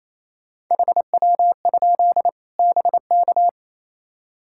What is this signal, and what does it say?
Morse code, heard as a keyed tone, about 28 wpm, 700 Hz. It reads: HW? BK
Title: Practice QSO - Exchange 1 - Beginning - 28wpm